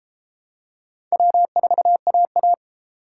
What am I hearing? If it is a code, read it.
Morse code, heard as a keyed tone, about 26 wpm, 700 Hz. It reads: W4AA
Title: Speed Racing - USA 1x2 Call Signs 22wpm